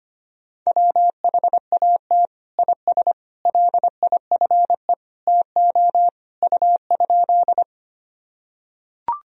Morse code, heard as a keyed tone, about 25 wpm, 700 Hz. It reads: WHAT IS LIFE TO U?
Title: Sentences from Top 300 Words and Abbreviations 25wpm